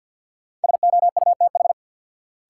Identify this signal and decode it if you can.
Morse code, heard as a keyed tone, about 53 wpm, 700 Hz. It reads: SOUTH
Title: Sentences from Top 400 Words - New Word Drill - 50wpm